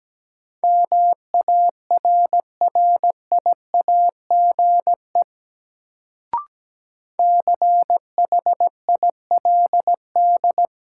Morse code, heard as a keyed tone, about 17 wpm, 700 Hz. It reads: MARRIAGE CHILD
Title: Sentences from Top 800 Words - New Word Drill - 17wpm